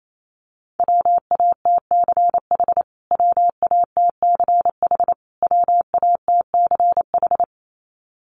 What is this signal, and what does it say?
Morse code, heard as a keyed tone, about 28 wpm, 700 Hz. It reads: WATCH WATCH WATCH
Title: Top 500 Words Repeated 28wpm